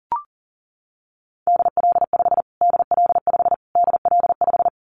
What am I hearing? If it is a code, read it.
Morse code, heard as a keyed tone, about 40 wpm, 700 Hz. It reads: DL5 DL5 DL5
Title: Callsign Prefixes for Top 25 DXCC - Repeated 3x - 40wpm